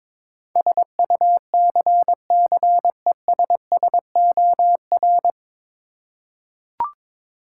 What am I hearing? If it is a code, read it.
Morse code, heard as a keyed tone, about 22 wpm, 700 Hz. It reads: SUCCESSOR
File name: 9 Letter Words - First 3 Letters Given - 22wpm